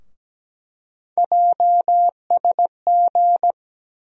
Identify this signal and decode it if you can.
Morse code, heard as a keyed tone, about 17 wpm, 700 Hz. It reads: JSG
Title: USA Call Sign Postfixes - 3 Letters 17wpm